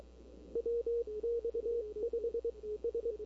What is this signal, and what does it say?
Morse code, heard as a keyed tone, about 22 wpm, spaced as for 18 wpm, 470 Hz. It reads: WX5S
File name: CW / 2005-k5tr arrl cw dx -21:00